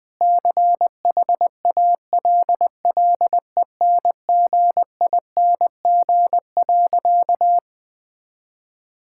Morse code, heard as a keyed tone, about 20 wpm, 700 Hz. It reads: CHALLENGING.
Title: Sentences with 5-Word Phrases - Spoken Beginning - 20wpm